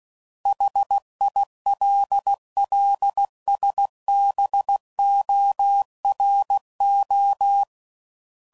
Morse code, heard as a keyed tone, about 16 wpm, 800 Hz. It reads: HILLSBORO
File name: CW Code Groups @15WPM / fromSM's / Cities